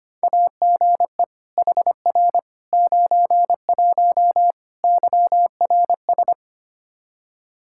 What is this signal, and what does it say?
Morse code, heard as a keyed tone, about 25 wpm, 700 Hz. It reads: AGE HR 91 YRS
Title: Practice QSO - Exchange 3 - More Info - 25wpm